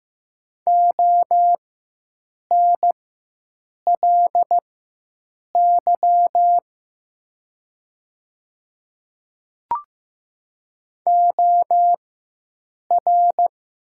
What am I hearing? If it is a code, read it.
Morse code, heard as a keyed tone, about 15 wpm, spaced as for 7 wpm, 700 Hz. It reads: ONLY OR
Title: 4x Character Spacing - Top 300 Words - 15wpm